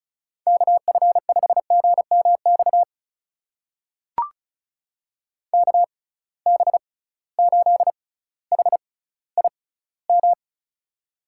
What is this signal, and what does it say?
Morse code, heard as a keyed tone, about 36 wpm, spaced as for 18 wpm, 700 Hz. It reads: KF5GMX KB8HIM
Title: USA Call Signs - 6 Characters - 6x Character Spacing - 35wpm